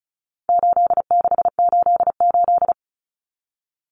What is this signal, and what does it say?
Morse code, heard as a keyed tone, about 34 wpm, 700 Hz. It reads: 8688
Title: Numbers - 4-Digit 35wpm